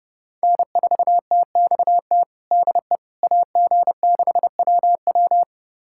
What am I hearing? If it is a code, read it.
Morse code, heard as a keyed tone, about 31 wpm, 700 Hz. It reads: N4TXT DE AG6WW
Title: USA Callsign DE Callsign - 5 Characters - 30wpm